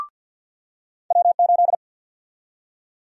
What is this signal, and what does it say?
Morse code, heard as a keyed tone, about 52 wpm, 700 Hz. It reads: W8
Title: USA Call Sign Prefixes - 50wpm